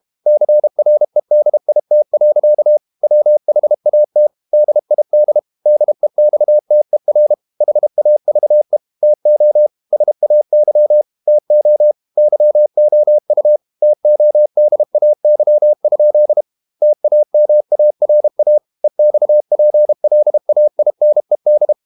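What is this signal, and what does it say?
Morse code, heard as a keyed tone, about 32 wpm, 600 Hz. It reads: CREDIT. WHAT DID DEXTER HAVE TO SAY TO YOU TODAY? TAMARA EXPLAINED
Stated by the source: Morse code practice at 32 wpm from CW med Gnister, reading Ken Follett's novel Never in English.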